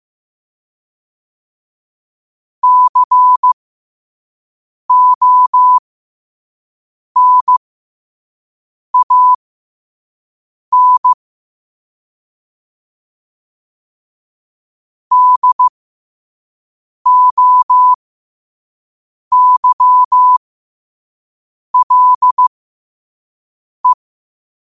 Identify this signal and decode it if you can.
Morse code, heard as a keyed tone, about 15 wpm, spaced as for 5 wpm, 1000 Hz. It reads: CONAN DOYLE
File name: qotdfiles / 5wpm